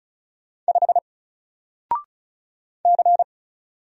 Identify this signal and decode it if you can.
Morse code, heard as a keyed tone, about 35 wpm, 700 Hz. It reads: 5 C
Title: Single Character (Letter:Number:Punctuation:Prosign) 35wpm